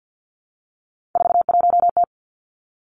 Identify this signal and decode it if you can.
Morse code, heard as a keyed tone, about 50 wpm, 700 Hz. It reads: 41T
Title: Three Challenging Letter-Number Combinations 50wpm